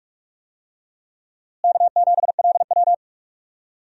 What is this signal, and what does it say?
Morse code, heard as a keyed tone, about 43 wpm, 700 Hz. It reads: K7LW
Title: USA Call Signs - 4 Characters - Repeated 3x - 45wpm